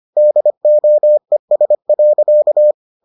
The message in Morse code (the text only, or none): DOES.